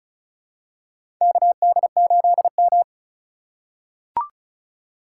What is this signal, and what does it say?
Morse code, heard as a keyed tone, about 34 wpm, 700 Hz. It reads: KD8M
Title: USA Call Signs - 4 Characters - 3x Character Spacing - 35wpm